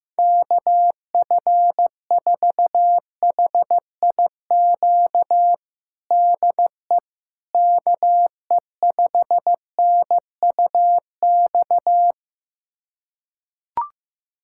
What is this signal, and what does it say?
Morse code, heard as a keyed tone, about 15 wpm, 700 Hz. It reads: KF4HIQ DE KE5NUX
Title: USA Callsign DE Callsign 15wpm